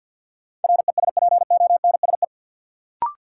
Morse code, heard as a keyed tone, about 47 wpm, 700 Hz. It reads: RESPONSE
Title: Sentences from Top 1400 Words - New Word Drill - 50wpm